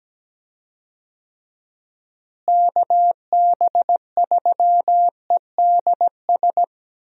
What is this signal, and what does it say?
Morse code, heard as a keyed tone, about 17 wpm, 700 Hz. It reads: KB3EDS
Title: USA Call Signs - 6 Characters - Repeated 3x - 17wpm